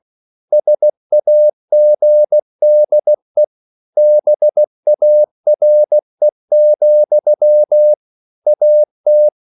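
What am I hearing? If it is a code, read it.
Morse code, heard as a keyed tone, about 16 wpm, 600 Hz. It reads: SAGDE BARE, AT
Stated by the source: Slow (QRS) Morse code at 16 wpm, from CW med Gnister.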